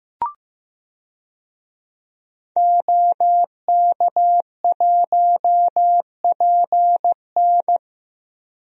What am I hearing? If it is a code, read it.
Morse code, heard as a keyed tone, about 15 wpm, 700 Hz. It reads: OK1PN
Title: DXCC-100 Call Signs - 5 Characters - 15wpm